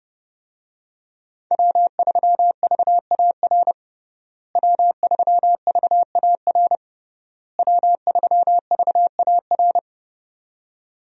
Morse code, heard as a keyed tone, about 30 wpm, 700 Hz. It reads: W3VAR W3VAR W3VAR
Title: USA Call Signs - 5 Characters - Repeated 3x - 30wpm